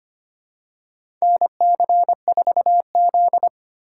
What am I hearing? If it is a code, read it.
Morse code, heard as a keyed tone, about 25 wpm, 700 Hz. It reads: NC4Z
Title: USA Call Signs - 4 Characters - 2x Character Spacing - 25wpm